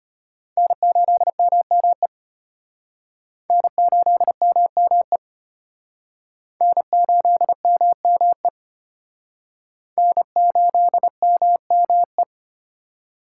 Morse code, heard as a keyed tone, about 33 wpm, 700 Hz. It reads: N8MME N8MME N8MME N8MME